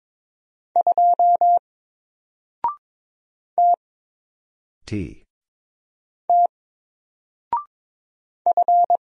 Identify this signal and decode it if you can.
Morse code, heard as a keyed tone, about 22 wpm, 700 Hz. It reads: 2 T T F
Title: Single Letter-Number 22wpm